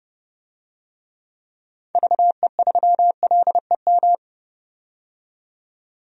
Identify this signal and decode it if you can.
Morse code, heard as a keyed tone, about 30 wpm, 700 Hz. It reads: VE3LEM
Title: DXCC-100 Call Signs - 6 Characters - 30wpm